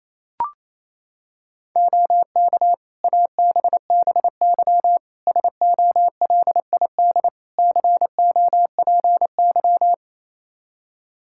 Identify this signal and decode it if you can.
Morse code, heard as a keyed tone, about 28 wpm, 700 Hz. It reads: OK ABBY SOLID COPY
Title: Practice QSO - Exchange 3 - More Info - 28wpm